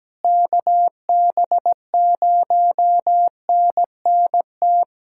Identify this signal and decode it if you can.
Morse code, heard as a keyed tone, about 17 wpm, 700 Hz. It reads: KB0NNT
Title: USA Callsign DE Callsign - 6 Characters - 17wpm